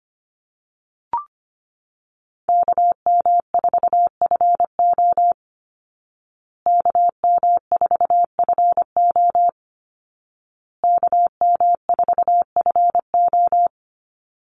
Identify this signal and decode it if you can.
Morse code, heard as a keyed tone, about 25 wpm, 700 Hz. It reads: KM4FO KM4FO KM4FO
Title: Call Signs Repeated 2x - 25wpm